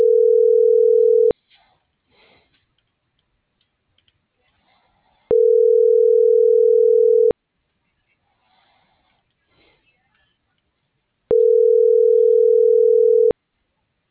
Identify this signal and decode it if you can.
Ringback tone (North America), 440 + 480 Hz.